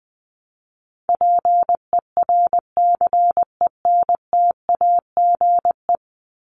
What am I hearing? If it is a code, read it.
Morse code, heard as a keyed tone, about 20 wpm, 700 Hz. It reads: PERCENTAGE
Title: Top 1001-1250 Words 20wpm